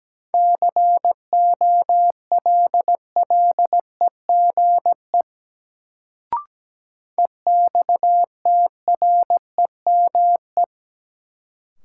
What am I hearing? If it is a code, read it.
Morse code, heard as a keyed tone, about 17 wpm, 700 Hz. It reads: COLLEGE EXTREME
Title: Sentences from Top 1300 Words - New Word Drill - 17wpm